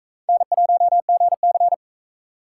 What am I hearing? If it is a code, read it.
Morse code, heard as a keyed tone, about 40 wpm, 700 Hz. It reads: N1GC